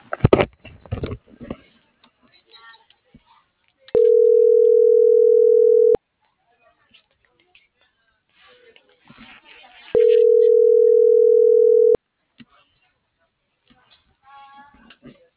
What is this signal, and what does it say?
Ringback tone (North America), 440 + 480 Hz.